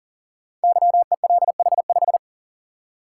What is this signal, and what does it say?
Morse code, heard as a keyed tone, about 42 wpm, 690 Hz. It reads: YELH5
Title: Code Groups - 5 Letter:Numbers - 40wpm